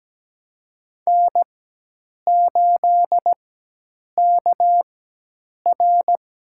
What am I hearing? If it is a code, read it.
Morse code, heard as a keyed tone, about 17 wpm, spaced as for 8 wpm, 700 Hz. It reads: N8KR
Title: USA Call Signs - 4 Characters - 4x Character Spacing - 17wpm